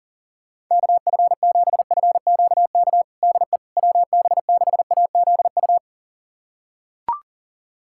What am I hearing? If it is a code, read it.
Morse code, heard as a keyed tone, about 40 wpm, 700 Hz. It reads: KF7FQK DE WB6AZU
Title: USA Callsign DE Callsign - 6 Characters - 40wpm